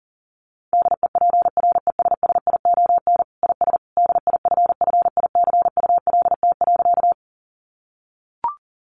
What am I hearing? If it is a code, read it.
Morse code, heard as a keyed tone, about 40 wpm, 700 Hz. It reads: DEPRESSION IS DIFFICULT.
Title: Sentences from Top 1300 Words 40wpm